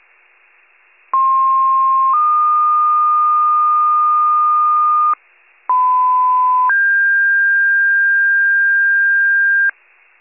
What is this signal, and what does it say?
Two-tone page, heard as a one-second tone, then a three-second tone, about 1060 then 1220 Hz, about 1010 then 1600 Hz. It is heard from 1.1 to 9.7 s.